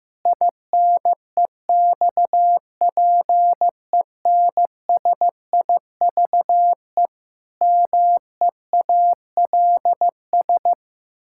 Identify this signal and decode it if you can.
Morse code, heard as a keyed tone, about 14 wpm, 700 Hz. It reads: INEXPENSIVE MEALS
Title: Pangram Challenge v2 - 15wpm